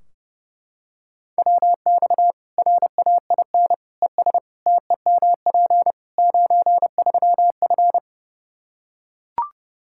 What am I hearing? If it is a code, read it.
Morse code, heard as a keyed tone, about 30 wpm, 700 Hz. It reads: WX RAIN ES TEMP 93F
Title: Practice QSO - Exchange 2 - Info - 30wpm